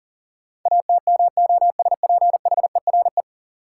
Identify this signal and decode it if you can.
Morse code, heard as a keyed tone, about 40 wpm, 700 Hz. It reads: ATMOSPHERE
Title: Sentences from Top 1600 Words - New Word Drill Alt - 40wpm